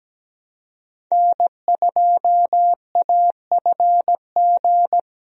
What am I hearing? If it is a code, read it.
Morse code, heard as a keyed tone, about 17 wpm, 700 Hz. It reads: N2AFG
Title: USA Call Signs - 5 Characters - Repeated 3x - 17wpm